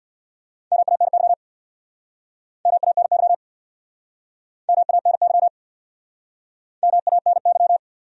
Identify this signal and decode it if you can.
Morse code, heard as a keyed tone, about 69 wpm, 700 Hz. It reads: MANY MANY MANY MANY